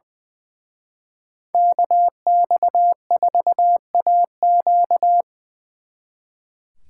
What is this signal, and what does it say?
Morse code, heard as a keyed tone, about 23 wpm, 700 Hz. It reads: KX4AQ
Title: Speed Racing - USA 2x2 Call Signs 20wpm